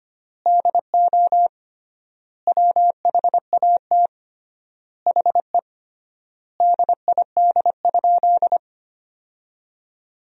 Morse code, heard as a keyed tone, about 25 wpm, 700 Hz. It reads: DO WHAT HE DID?
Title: 3x Word-Spacing - Sentences from Top 100 Words - 25wpm